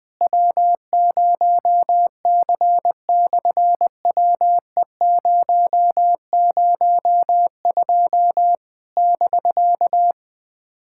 Morse code, heard as a keyed tone, about 20 wpm, 700 Hz. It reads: W0C/WE002 <BK>
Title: SOTA QSO Practice 20wpm